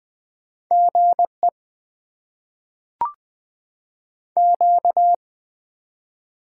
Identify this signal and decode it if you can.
Morse code, heard as a keyed tone, about 20 wpm, 700 Hz. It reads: GE Q
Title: Top 20 QSO Elements - 8x Character Spacing - 20wpm